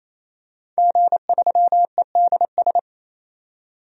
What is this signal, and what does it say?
Morse code, heard as a keyed tone, about 28 wpm, 700 Hz. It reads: G3EDS
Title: DXCC-100 Call Signs - 5 Characters - 28wpm